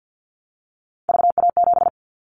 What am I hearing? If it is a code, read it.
Morse code, heard as a keyed tone, about 50 wpm, 700 Hz. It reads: VA7
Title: Callsign Prefixes for Top 25 DXCC - 50wpm